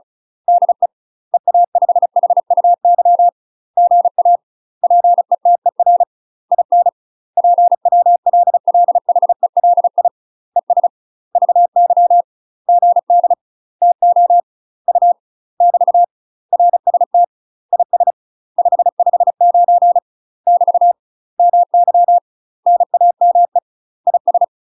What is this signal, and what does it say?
Morse code, heard as a keyed tone, about 36 wpm, 700 Hz. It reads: DE EA5HUY GA PETER IN PWLLHELI ES VY GD TO U <BT> RST IS 559 <BT> MY NAME IS
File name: UK MASTER QSOs TRAINING / 35WPM / UK MASTER QSOs 0026